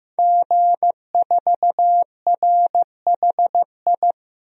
Morse code, heard as a keyed tone, about 16 wpm, 700 Hz. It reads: G4RHI
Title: DXCC-100 Call Signs - 5 Characters - 15wpm